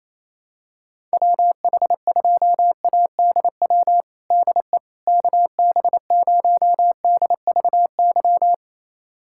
Morse code, heard as a keyed tone, about 28 wpm, 700 Hz. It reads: WH2ADW DE KB0DVY
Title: USA Callsign DE Callsign - 6 Characters - 28wpm